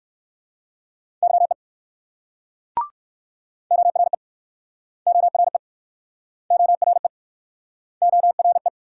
Morse code, heard as a keyed tone, about 60 wpm, 700 Hz. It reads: YE ORE ORE ORE ORE